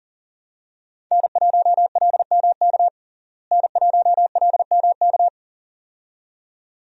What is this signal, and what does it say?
Morse code, heard as a keyed tone, about 40 wpm, 700 Hz. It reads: N1LMK N1LMK
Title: USA Call Signs - 5 Characters - Repeated 2x - 40wpm